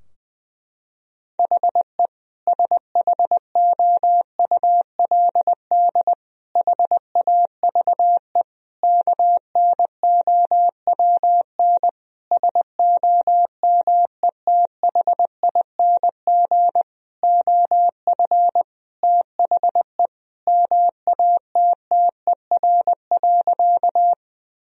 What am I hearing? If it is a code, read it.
Morse code, heard as a keyed tone, about 20 wpm, 700 Hz. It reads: HE SHOULD HAVE KNOWN SOMETHING OF THE MATTER.
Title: Sentences from Top 1000 Words - Review Part 3 - 20wpm